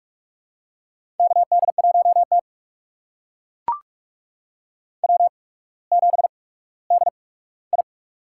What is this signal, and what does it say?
Morse code, heard as a keyed tone, about 44 wpm, spaced as for 19 wpm, 700 Hz. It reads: KD1T W7DI
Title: USA Call Signs - 4 Characters - 8x Character Spacing - 45wpm